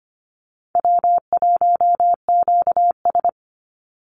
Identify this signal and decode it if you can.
Morse code, heard as a keyed tone, about 25 wpm, 700 Hz. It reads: W1QS
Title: Easy Call Signs - 25wpm